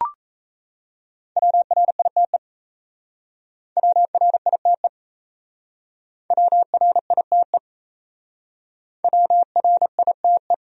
Morse code, heard as a keyed tone, about 37 wpm, 700 Hz. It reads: WRITE WRITE WRITE WRITE